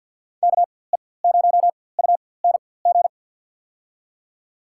Morse code, heard as a keyed tone, about 52 wpm, spaced as for 24 wpm, 700 Hz. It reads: KE0UNG
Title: USA Call Signs - 4x Character Spacing - 50wpm